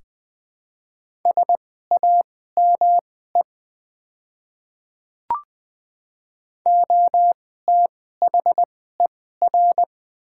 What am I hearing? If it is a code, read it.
Morse code, heard as a keyed tone, about 20 wpm, spaced as for 14 wpm, 700 Hz. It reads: SAME OTHER
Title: 2x Character Spacing - Top 200 Words - 20wpm